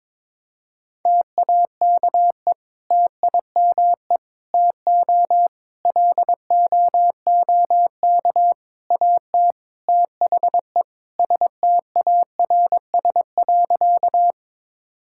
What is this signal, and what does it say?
Morse code, heard as a keyed tone, about 22 wpm, 700 Hz. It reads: TAKE TIME TO LOOK AT THE STARS.
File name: practice / 22wpm / Sentences with 3-Word Phrases - 22wpm